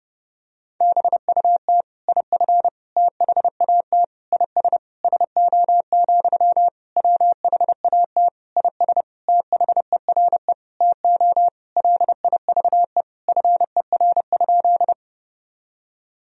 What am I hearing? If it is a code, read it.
Morse code, heard as a keyed tone, about 30 wpm, 700 Hz. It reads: BUT IF THAT IS SO, WHAT IS THERE TO LIVE FER?
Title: Sentences from Top 200 Words and Abbreviations 30wpm